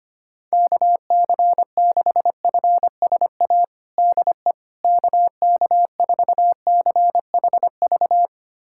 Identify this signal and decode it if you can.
Morse code, heard as a keyed tone, about 25 wpm, 700 Hz. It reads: KC6FSA DE KK4CHV
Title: USA Callsign DE Callsign 25wpm